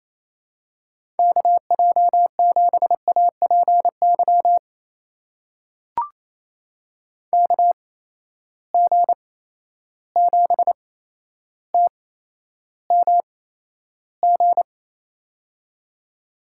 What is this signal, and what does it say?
Morse code, heard as a keyed tone, about 28 wpm, spaced as for 12 wpm, 700 Hz. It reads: KJ7APY KG7TMG
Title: USA Call Signs - 8x Character Spacing - 28wpm